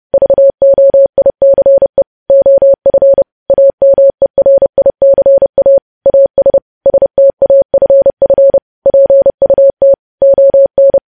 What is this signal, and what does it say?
Morse code, heard as a keyed tone, about 30 wpm, 550 Hz. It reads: VOICE OF AMERICA AS STAFF PUT ON
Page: News Headlines in Morse Code at 30 WPM